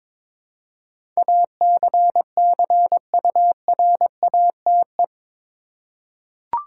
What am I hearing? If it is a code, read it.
Morse code, heard as a keyed tone, about 21 wpm, 700 Hz. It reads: ACCURATE
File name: practice / 22wpm / Eight Letter Words - 22wpm